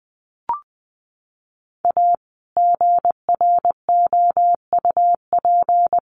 Morse code, heard as a keyed tone, about 20 wpm, 700 Hz. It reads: A GROUP
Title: Sets of 3 Words 20wpm